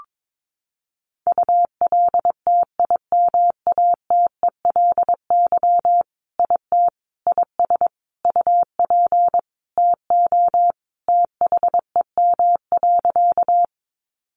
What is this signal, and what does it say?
Morse code, heard as a keyed tone, about 22 wpm, 700 Hz. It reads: ULTIMATELY IT IS UP TO THEM.
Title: Sentences from Top 1200 Words 22wpm